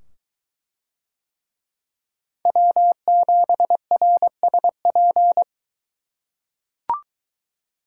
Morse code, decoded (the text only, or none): W7RSP